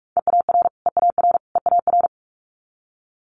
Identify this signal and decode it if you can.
Morse code, heard as a keyed tone, about 46 wpm, 700 Hz. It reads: EAR EAR EAR
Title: Multi-letter Phonemes Repeated 3x 45wpm